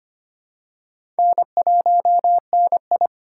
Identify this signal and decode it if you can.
Morse code, heard as a keyed tone, about 25 wpm, 700 Hz. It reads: N1NI
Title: USA Call Signs - 4 Characters - 4x Character Spacing - 25wpm